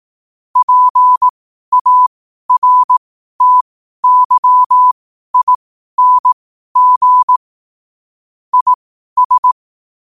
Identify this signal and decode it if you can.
Morse code, heard as a keyed tone, about 18 wpm, spaced as for 12 wpm, 1000 Hz. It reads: PARTYING IS